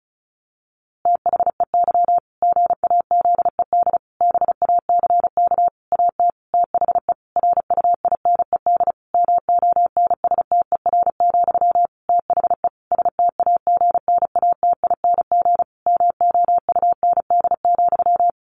Morse code, heard as a keyed tone, about 34 wpm, 700 Hz. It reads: THEY GAZED BACK AT THE RUINED MONSTER, THE STAGNATING MOUND,
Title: A Sound Of Thunder 35wpm